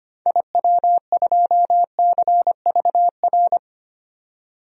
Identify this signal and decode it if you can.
Morse code, heard as a keyed tone, about 25 wpm, 700 Hz. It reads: IW2CVR
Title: DXCC-100 Call Signs - 6 Characters - 25wpm